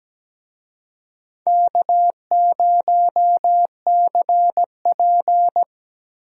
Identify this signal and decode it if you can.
Morse code, heard as a keyed tone, about 17 wpm, 700 Hz. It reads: K0CP
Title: USA Call Signs - 4 Characters - Repeated 2x - 17wpm